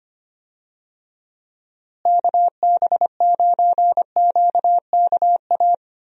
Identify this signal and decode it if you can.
Morse code, heard as a keyed tone, about 25 wpm, 700 Hz. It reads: KB9QKA
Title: USA Call Signs - 6 Characters - Repeated 2x - 25wpm